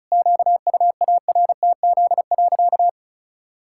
Morse code, heard as a keyed tone, about 36 wpm, 700 Hz. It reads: QUARTZ.
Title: Pangram Challenge v2 - 35wpm